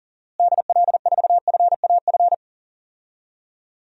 Morse code, heard as a keyed tone, about 40 wpm, 700 Hz. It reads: DL4FAF
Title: DXCC-100 Call Signs - 6 Characters - Repeated 3x - 40wpm